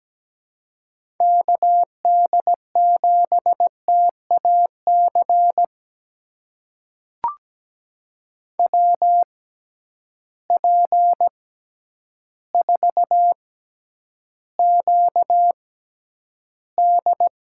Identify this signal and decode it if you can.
Morse code, heard as a keyed tone, about 17 wpm, spaced as for 9 wpm, 700 Hz. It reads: KD7TAC WP4QD
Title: USA Call Signs - 6 Characters - 6x Character Spacing - 17wpm